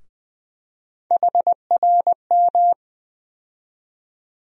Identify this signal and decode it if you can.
Morse code, heard as a keyed tone, about 21 wpm, 700 Hz. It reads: HRM
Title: USA Call Sign Postfixes - 3 Letters 20wpm